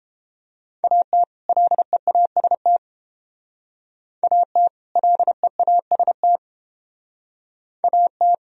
Morse code, heard as a keyed tone, about 30 wpm, 700 Hz. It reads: AT LEAST AT LEAST AT